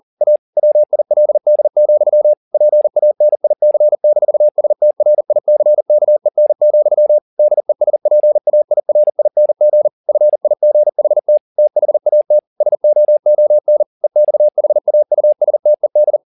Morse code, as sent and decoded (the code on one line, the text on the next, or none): .- / .-- .. .-.. -.. --..-- / .--. .- -. .. -.-. -....- ... - .-. .. -.-. -.- . -. --..-- / -.. ... .--. .- .. .-. .. -. --. / ..-. .. --. .... - / - .... .- - / ... --- --- -. / . -..- .... .- ..- ... - . -..
A WILD, PANIC-STRICKEN, DESPAIRING FIGHT THAT SOON EXHAUSTED